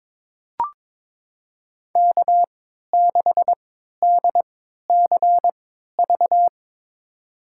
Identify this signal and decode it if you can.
Morse code, heard as a keyed tone, about 22 wpm, spaced as for 13 wpm, 700 Hz. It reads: K6DCV
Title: USA Call Signs - 3x Character Spacing - 22wpm